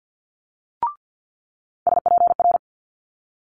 Morse code, heard as a keyed tone, about 50 wpm, 700 Hz. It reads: SPR
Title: Multi-letter Phonemes Quick Fire 50wpm